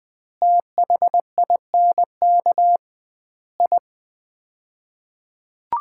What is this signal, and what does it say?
Morse code, heard as a keyed tone, about 20 wpm, 700 Hz. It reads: THINK I
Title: 2x Word-Spacing - Sets of 5 Words - 20wpm